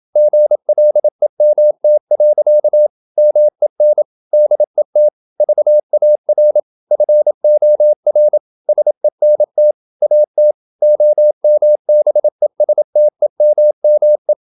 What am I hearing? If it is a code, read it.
Morse code, heard as a keyed tone, about 27 wpm, 600 Hz. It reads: GLEMT. MEN DET VAR FOR SENT AT OMBESTEMME
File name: Ferie QRQ 27wpm